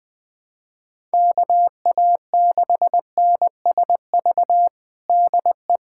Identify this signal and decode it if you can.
Morse code, heard as a keyed tone, about 19 wpm, 700 Hz. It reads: KA6NSV DE
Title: USA Callsign DE Callsign 20wpm